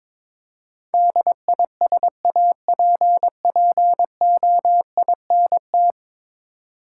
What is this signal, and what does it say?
Morse code, heard as a keyed tone, about 22 wpm, 700 Hz. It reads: DISAPPOINT
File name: practice / 22wpm / 10 Letter Words - First 3 Letters Given - 22wpm